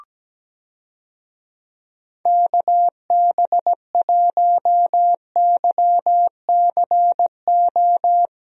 Morse code, heard as a keyed tone, about 17 wpm, 700 Hz. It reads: KB1YCO
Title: USA Call Signs - 6 Characters - Repeated 2x - 17wpm